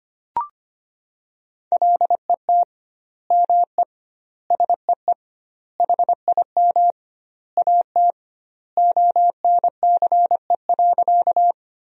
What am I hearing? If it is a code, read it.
Morse code, heard as a keyed tone, about 25 wpm, 700 Hz. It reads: LET ME SEE HIM AT ONCE.
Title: 2x Word-Spacing - Sentences from Top 300 Words - 25wpm